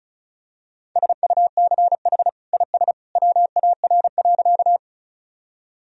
Morse code, heard as a keyed tone, about 34 wpm, 700 Hz. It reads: SUCH IS WAR.
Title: Sentences from Top 1000 Words - Review Part 2 - 35wpm